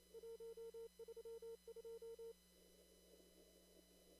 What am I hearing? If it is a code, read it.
Morse code, heard as a keyed tone, about 28 wpm, 460 Hz. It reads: 132
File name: CW / kt5j wpx cw-op-k5pi